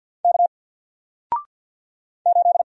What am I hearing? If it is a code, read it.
Morse code, heard as a keyed tone, about 48 wpm, 700 Hz. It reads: K 8
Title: Single Letter-Number Alt 50wpm